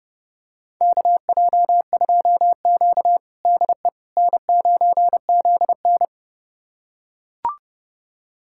Morse code, heard as a keyed tone, about 30 wpm, 700 Hz. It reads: KJ2Q DE N9ZN
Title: USA Callsign DE Callsign - 4 Characters - 30wpm